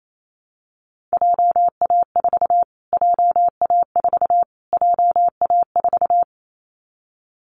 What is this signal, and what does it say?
Morse code, heard as a keyed tone, about 28 wpm, 700 Hz. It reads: JA4 JA4 JA4
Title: Callsign Prefixes for Top 25 DXCC - Repeated 3x - 28wpm